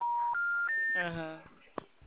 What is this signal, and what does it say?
Special information tone, with speech.